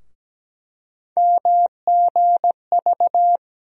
Morse code, heard as a keyed tone, about 17 wpm, 700 Hz. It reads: MGV
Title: USA Call Sign Postfixes - 3 Letters 17wpm